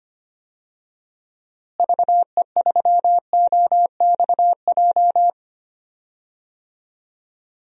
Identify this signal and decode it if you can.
Morse code, heard as a keyed tone, about 25 wpm, 700 Hz. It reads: VE3OXJ
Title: DXCC-100 Call Signs - 6 Characters - Repeated 3x - 25wpm